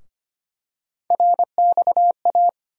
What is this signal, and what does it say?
Morse code, heard as a keyed tone, about 25 wpm, 700 Hz. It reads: RXA
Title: USA Call Sign Postfixes - 3 Letters 25wpm